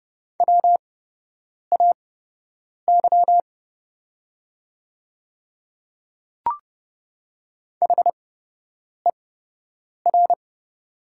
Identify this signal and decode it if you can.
Morse code, heard as a keyed tone, about 30 wpm, spaced as for 8 wpm, 700 Hz. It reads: WAY HER
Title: 8x Character Spacing - Top 500 Words - 30wpm